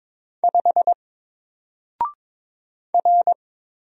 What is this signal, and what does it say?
Morse code, heard as a keyed tone, about 22 wpm, 700 Hz. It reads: 5 R
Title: Single Letter-Number 22wpm